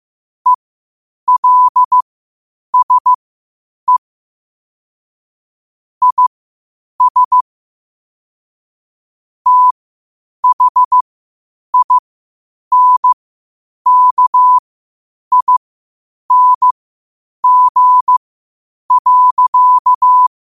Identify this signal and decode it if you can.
Morse code, heard as a keyed tone, about 15 wpm, spaced as for 8 wpm, 1000 Hz. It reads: ELSE IS THINKING.
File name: qotdfiles / 8wpm